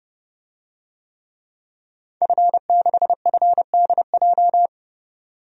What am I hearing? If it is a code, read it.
Morse code, heard as a keyed tone, about 29 wpm, 700 Hz. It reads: F6FDJ
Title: DXCC-100 Call Signs - 5 Characters - 30wpm